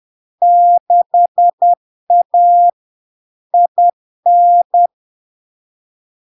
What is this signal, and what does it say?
Morse code, heard as a keyed tone, about 10 wpm, 700 Hz. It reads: 6A IN
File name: LARK-FD-2022-Log-in-CW-10wpm-700-Hz